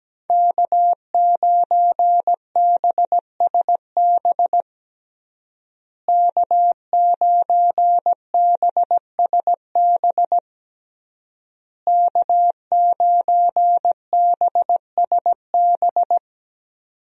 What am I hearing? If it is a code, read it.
Morse code, heard as a keyed tone, about 17 wpm, 700 Hz. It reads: K9BSB K9BSB K9BSB
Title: USA Call Signs - 5 Characters - Repeated 3x - 17wpm